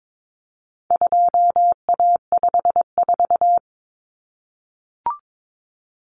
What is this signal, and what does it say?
Morse code, heard as a keyed tone, about 22 wpm, 700 Hz. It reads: 2A54
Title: Common Cut Numbers - 4-Digit 22wpm